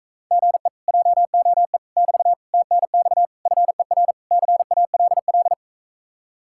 Morse code, heard as a keyed tone, about 45 wpm, 700 Hz. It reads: GE JOE <BT> TNX FER CALL